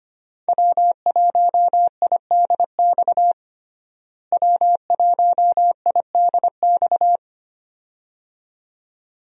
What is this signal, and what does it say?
Morse code, heard as a keyed tone, about 25 wpm, 700 Hz. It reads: W1IDX W1IDX
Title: USA Call Signs - 5 Characters - Repeated 2x - 25wpm